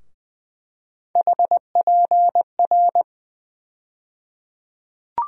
Morse code, heard as a keyed tone, about 20 wpm, 700 Hz. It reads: HPR
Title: USA Call Sign Postfixes - 3 Letters 20wpm